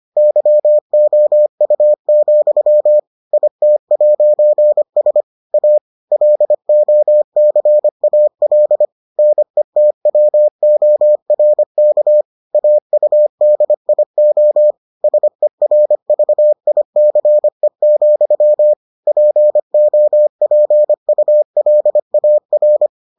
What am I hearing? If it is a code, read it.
Morse code, heard as a keyed tone, about 25 wpm, 600 Hz. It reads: YOU, IT'S A LOCAL NETWORK AUDIO SERVICE, POPULAR